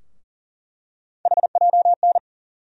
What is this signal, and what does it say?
Morse code, heard as a keyed tone, about 40 wpm, 700 Hz. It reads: HJN
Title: USA Call Sign Postfixes - 3 Letters 40wpm